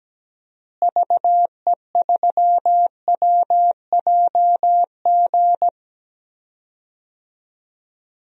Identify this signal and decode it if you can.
Morse code, heard as a keyed tone, about 17 wpm, 700 Hz. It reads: VE3WJG
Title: DXCC-100 Call Signs - 6 Characters - Repeated 2x - 17wpm